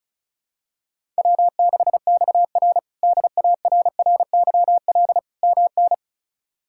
Morse code, heard as a keyed tone, about 34 wpm, 700 Hz. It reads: W6XR DARRYL MN
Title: K1USN SST QSO Practice - 35wpm